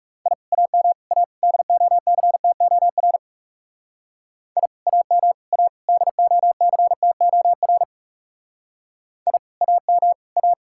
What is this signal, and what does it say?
Morse code, heard as a keyed tone, about 36 wpm, 700 Hz. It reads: I AM A DOCTOR I AM A DOCTOR I AM A
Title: Speed-Racing Common QSO Phrases Encore - 30wpm